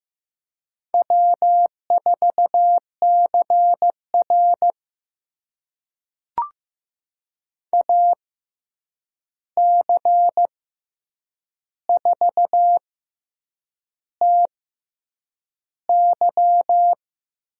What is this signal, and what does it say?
Morse code, heard as a keyed tone, about 15 wpm, spaced as for 7 wpm, 700 Hz. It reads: W4CR AC4TY
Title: USA Call Signs - 6x Character Spacing - 15wpm